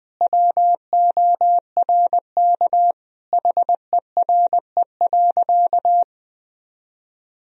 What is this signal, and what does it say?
Morse code, heard as a keyed tone, about 20 wpm, 700 Hz. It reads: WORK HERE.
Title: Sentences from Top 1000 Words - Review Part 4 - 20wpm